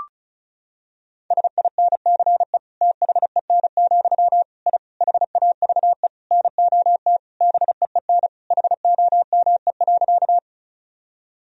Morse code, heard as a keyed tone, about 36 wpm, 700 Hz. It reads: SINCE THEN, I HAVE NOT BEEN HOME.
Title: Sentences from Top 1000 Words - Review Part 1 - 35wpm